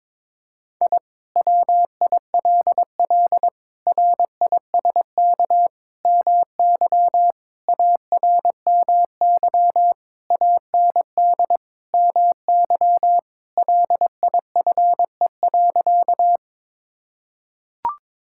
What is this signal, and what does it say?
Morse code, heard as a keyed tone, about 22 wpm, 700 Hz. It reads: I WILL RISK MY ARMY AND MY LIFE.
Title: Sentences from Top 1000 Words - Review Part 5 - 22wpm